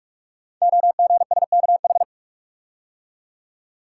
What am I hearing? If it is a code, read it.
Morse code, heard as a keyed tone, about 44 wpm, 690 Hz. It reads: OGSKH
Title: Code Groups - 5 Letters - 45wpm